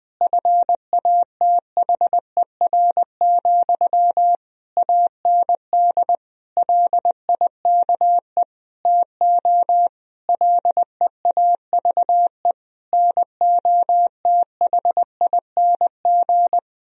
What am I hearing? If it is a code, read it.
Morse code, heard as a keyed tone, about 20 wpm, 700 Hz. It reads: FATHER, AND LIKE TO LEAVE NOTHING
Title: Sentences from Top 1000 Words - Review Part 1 - 20wpm